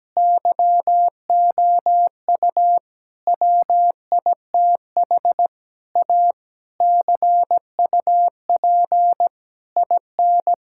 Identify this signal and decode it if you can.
Morse code, heard as a keyed tone, about 17 wpm, 700 Hz. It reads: YOU WITH A CUP IN
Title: Sentences from Top 1600 Words 17wpm